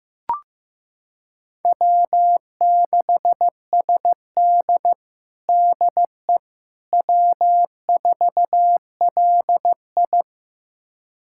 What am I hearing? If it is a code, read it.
Morse code, heard as a keyed tone, about 15 wpm, 700 Hz. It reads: W6SD DE W4LI
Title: USA Callsign DE Callsign - 4 Characters - 15wpm